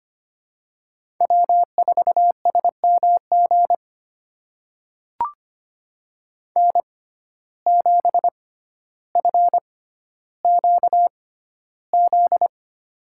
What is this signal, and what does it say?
Morse code, heard as a keyed tone, about 25 wpm, spaced as for 13 wpm, 700 Hz. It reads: W4SMG N7FQZ
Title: USA Call Signs - 5 Characters - 6x Character Spacing - 25wpm